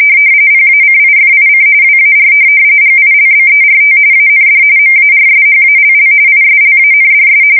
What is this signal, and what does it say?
RTTY, about 45.45 baud. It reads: PACIFIC WAR GILBERT AND MARSHALL ISLANDS
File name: fsk170 / USS-Massachusetts